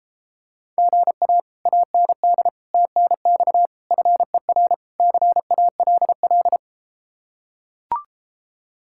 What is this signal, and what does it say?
Morse code, heard as a keyed tone, about 33 wpm, 700 Hz. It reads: GA AND TNX FER CALL